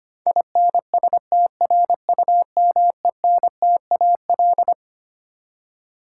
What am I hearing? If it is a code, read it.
Morse code, heard as a keyed tone, about 25 wpm, 700 Hz. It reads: INSTRUMENTAL
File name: Twelve Letter Words - 25wpm